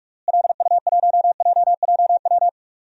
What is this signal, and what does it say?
Morse code, heard as a keyed tone, about 46 wpm, 700 Hz. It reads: LU1JJW
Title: DXCC-100 Call Signs - 6 Characters - Repeated 3x - 45wpm